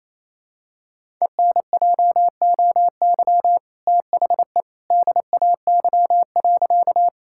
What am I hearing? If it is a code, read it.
Morse code, heard as a keyed tone, about 28 wpm, 700 Hz. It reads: ENJOY THE DAY.
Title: Sentences from Top 1000 Words - Review Part 5 - 28wpm